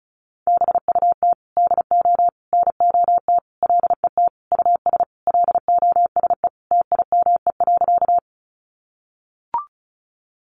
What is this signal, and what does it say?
Morse code, heard as a keyed tone, about 34 wpm, 700 Hz. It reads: BUT DO NOT LET US LOSE TIME.
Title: Sentences from Top 1100 Words 35wpm